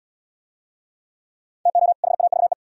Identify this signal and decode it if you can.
Morse code, heard as a keyed tone, about 70 wpm, 700 Hz. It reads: TO HAVE